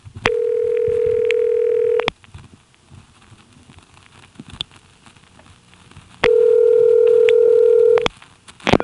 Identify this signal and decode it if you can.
Ringback tone (North America), 440 + 480 Hz.